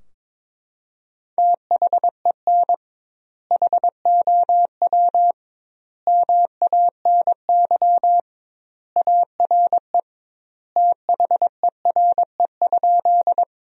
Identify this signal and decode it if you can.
Morse code, heard as a keyed tone, about 22 wpm, 700 Hz. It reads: THEN HOW MANY ARE THERE?
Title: 2x Word-Spacing - Sentences from Top 100 Words - 22wpm